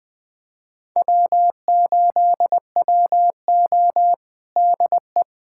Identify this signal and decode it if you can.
Morse code, heard as a keyed tone, about 20 wpm, 700 Hz. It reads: W8WO DE
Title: USA Callsign DE Callsign - 4 Characters - 20wpm